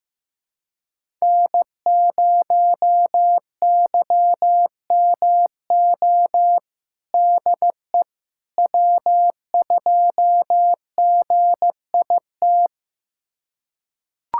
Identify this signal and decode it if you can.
Morse code, heard as a keyed tone, about 15 wpm, 700 Hz. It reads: N0YMO DE W2GIT
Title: USA Callsign DE Callsign - 5 Characters - 15wpm